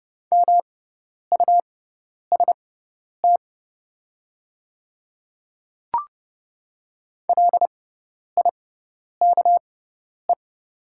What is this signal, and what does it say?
Morse code, heard as a keyed tone, about 30 wpm, spaced as for 10 wpm, 700 Hz. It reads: MUST LIKE
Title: 6x Character Spacing - Top 200 Words - 30wpm